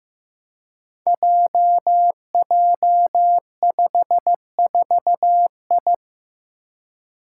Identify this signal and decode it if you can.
Morse code, heard as a keyed tone, about 15 wpm, 700 Hz. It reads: JJ54I
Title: Five Challenging Letter-Number Combinations 15wpm